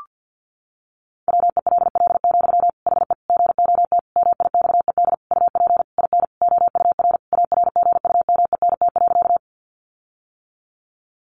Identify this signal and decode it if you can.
Morse code, heard as a keyed tone, about 50 wpm, 700 Hz. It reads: WELL, HE GOT MIXED UP IN OUR ARGUMENT.
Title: Sentences from Top 1500 Words 50wpm